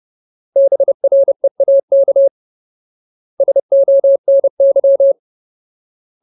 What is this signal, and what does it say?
Morse code, heard as a keyed tone, about 30 wpm, 550 Hz. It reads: BREAK SONY
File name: Ultimate Word List / 30WPM Ultimate